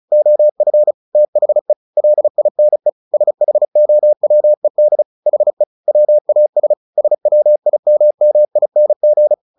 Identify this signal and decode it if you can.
Morse code, heard as a keyed tone, about 34 wpm, 600 Hz. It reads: OF THE LINE SHOWED HE WAS SWIMMING
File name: the old man and the sea / the old man and the sea 1 35wpm